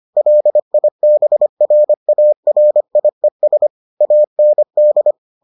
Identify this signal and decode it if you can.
Morse code, heard as a keyed tone, about 25 wpm, 600 Hz. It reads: LIBRARIES AND